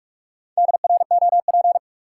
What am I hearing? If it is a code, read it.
Morse code, heard as a keyed tone, about 45 wpm, 700 Hz. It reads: DROP
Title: Sentences from Top 1400 Words - New Word Drill Alt - 45wpm